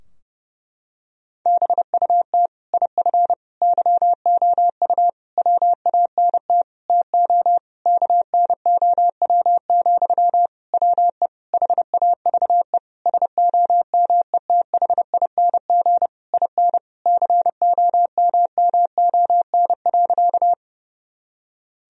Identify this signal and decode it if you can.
Morse code, heard as a keyed tone, about 30 wpm, 700 Hz. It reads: BUT IF YOU WANT TO KNOW, WE HAVE SOMETHING IN COMMON.
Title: Sentences from Top 1000 Words - Review Part 3 - 30wpm